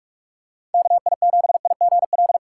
Morse code, heard as a keyed tone, about 46 wpm, 700 Hz. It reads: KI7IGL
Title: USA Call Signs - 2x Character Spacing - 45wpm